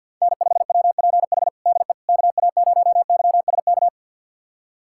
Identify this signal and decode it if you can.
Morse code, heard as a keyed tone, about 52 wpm, 700 Hz. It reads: N5WPH DE KA0YSK
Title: USA Callsign DE Callsign 50wpm